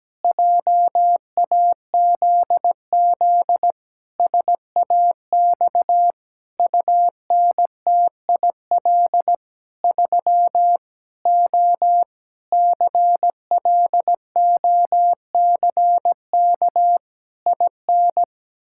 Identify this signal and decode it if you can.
Morse code, heard as a keyed tone, about 17 wpm, 700 Hz. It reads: JAZZ SAX UNTIL 3 O CLOCK IN
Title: Pangram Challenge v2 - 17wpm